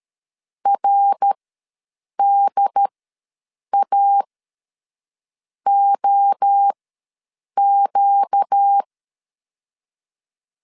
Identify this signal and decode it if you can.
Morse code, heard as a keyed tone, about 13 wpm, spaced as for 6 wpm, 800 Hz. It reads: RDAOQ
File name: LEZIONI / 5 LBVC / 5 LB ws 12wpm 3034